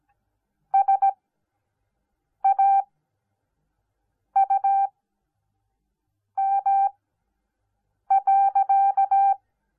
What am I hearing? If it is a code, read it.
Morse code, heard as a keyed tone, about 17 wpm, spaced as for 5 wpm, 780 Hz. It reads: SAUM.